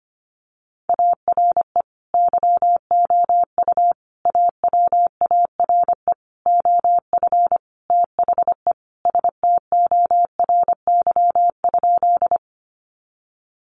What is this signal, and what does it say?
Morse code, heard as a keyed tone, about 25 wpm, 700 Hz. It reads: ARE YOU AWARE OF THE STORY?
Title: Sentences from Top 1200 Words 25wpm